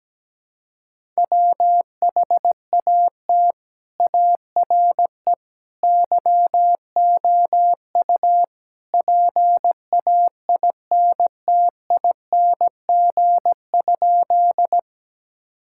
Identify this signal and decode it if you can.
Morse code, heard as a keyed tone, about 17 wpm, 700 Hz. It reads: WHAT ARE YOU PAINTING?
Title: Sentences from Top 1000 Words - Review Part 6 - 17wpm